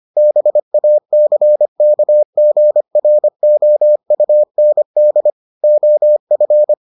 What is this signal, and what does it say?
Morse code, heard as a keyed tone, about 25 wpm, 600 Hz. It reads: BACKGROUND OF